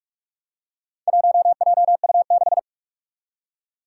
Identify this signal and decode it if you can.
Morse code, heard as a keyed tone, about 42 wpm, 700 Hz. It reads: 1JU6
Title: Four Challenging Letter-Number Combinations 45wpm